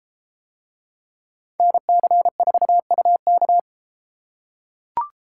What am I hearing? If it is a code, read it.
Morse code, heard as a keyed tone, about 32 wpm, 700 Hz. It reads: NC4UK